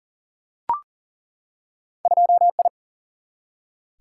Morse code, heard as a keyed tone, about 38 wpm, 700 Hz. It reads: 2I
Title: Single-Letter Single-Number 40wpm